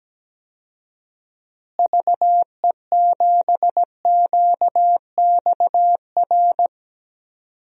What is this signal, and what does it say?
Morse code, heard as a keyed tone, about 17 wpm, 700 Hz. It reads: VE7QXR
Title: DXCC-100 Call Signs - 6 Characters - Repeated 2x - 17wpm